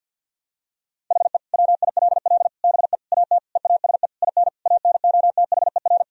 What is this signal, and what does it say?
Morse code, heard as a keyed tone, about 51 wpm, 700 Hz. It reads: HE WILL BE AT EASE IN ANOTHER
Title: Sentences from Top 1000 Words - Review Part 1 - 50wpm